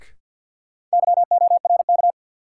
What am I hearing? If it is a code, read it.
Morse code, heard as a keyed tone, about 49 wpm, 700 Hz. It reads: YORK
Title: Sets of 2 Words - New Word Drill - 50wpm